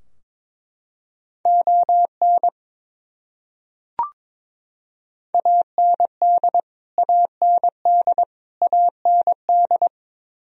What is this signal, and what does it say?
Morse code, heard as a keyed tone, about 22 wpm, 700 Hz. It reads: ON AND AND AND
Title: Top 100 Words Repeated 22wpm